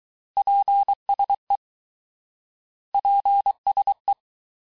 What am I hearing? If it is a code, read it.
Morse code, heard as a keyed tone, about 23 wpm, 800 Hz. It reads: PSE PSE
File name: cours-du-lundi files / cours-1852-abreviations-vitesse-17MPM-puis-22MPM